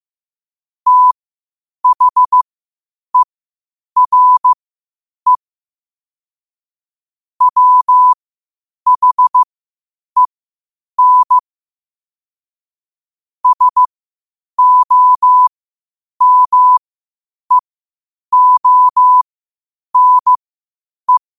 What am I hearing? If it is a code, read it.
Morse code, heard as a keyed tone, about 15 wpm, spaced as for 8 wpm, 1000 Hz. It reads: THERE WHEN SOMEONE
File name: qotdfiles / 8wpm